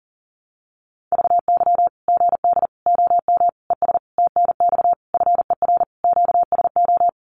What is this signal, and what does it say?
Morse code, heard as a keyed tone, about 40 wpm, 700 Hz. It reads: VY GD OM ES TNX FER QSO
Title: Common QSO Phrases 40wpm